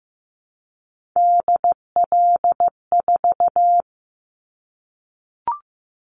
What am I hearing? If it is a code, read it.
Morse code, heard as a keyed tone, about 15 wpm, 700 Hz. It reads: DL4
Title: Callsign Prefixes for Top 25 DXCC - Repeated 3x - 15wpm